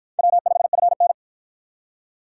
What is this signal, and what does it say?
Morse code, heard as a keyed tone, about 48 wpm, 700 Hz. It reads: W5FN